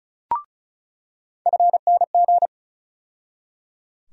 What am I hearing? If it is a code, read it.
Morse code, heard as a keyed tone, about 36 wpm, 700 Hz. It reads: FNG
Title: USA Call Sign Postfixes - 3 Letters 35wpm